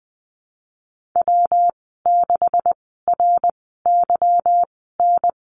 Morse code, heard as a keyed tone, about 20 wpm, spaced as for 15 wpm, 700 Hz. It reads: W6RYN
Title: 161 Lesson 26g - Letter g - Call Signs 20wpm